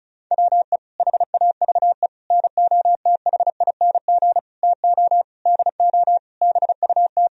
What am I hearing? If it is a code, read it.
Morse code, heard as a keyed tone, about 36 wpm, 700 Hz. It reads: WE HAVE NOTHING TO DO BUT
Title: Sentences from Top 1000 Words - Review Part 1 - 35wpm